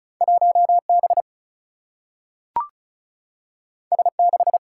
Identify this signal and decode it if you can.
Morse code, heard as a keyed tone, about 36 wpm, 700 Hz. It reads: 1 B S 6
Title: Single-Letter Single-Number 35wpm